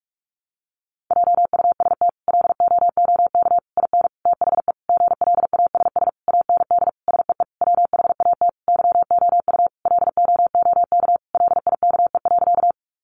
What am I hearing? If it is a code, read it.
Morse code, heard as a keyed tone, about 44 wpm, 700 Hz. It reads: JUST LOOK IN THE GLASS AND SEE WHAT YOU LOOK LIKE.
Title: Sentences from Top 1300 Words 45wpm